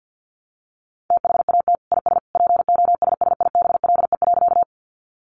Morse code, heard as a keyed tone, about 50 wpm, 700 Hz. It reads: THAT IS POSSIBLE.
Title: Sentences from Top 500 Words 50wpm